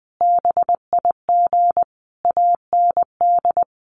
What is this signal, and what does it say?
Morse code, heard as a keyed tone, about 20 wpm, 700 Hz. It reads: BIG AND
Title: Sentences from Top 1300 Words 20wpm